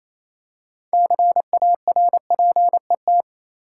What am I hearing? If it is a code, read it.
Morse code, heard as a keyed tone, about 28 wpm, 700 Hz. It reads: CARPET
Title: Six Letter Words - 28wpm